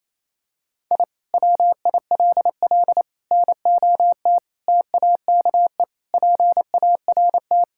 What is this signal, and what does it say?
Morse code, heard as a keyed tone, about 28 wpm, 700 Hz. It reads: I WILL NOT TAKE PART
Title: Sentences from Top 1000 Words - Review Part 6 - 28wpm